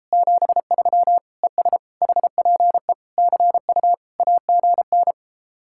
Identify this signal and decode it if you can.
Morse code, heard as a keyed tone, about 33 wpm, 700 Hz. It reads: 73 ES HPE CU AGN